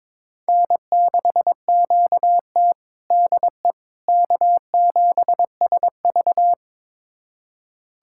Morse code, heard as a keyed tone, about 22 wpm, 700 Hz. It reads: N6QT DE K7SV
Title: USA Callsign DE Callsign - 4 Characters - 22wpm